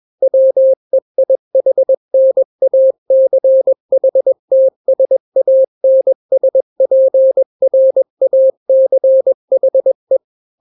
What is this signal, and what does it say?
Morse code, heard as a keyed tone, about 21 wpm, 530 Hz. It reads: WEIHNACHTSANSPRACHE
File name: daily / 20wpm